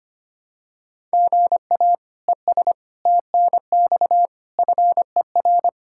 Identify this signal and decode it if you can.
Morse code, heard as a keyed tone, about 25 wpm, 700 Hz. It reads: GA ES TNX FER
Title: Practice QSO - Exchange 1 - Beginning - 25wpm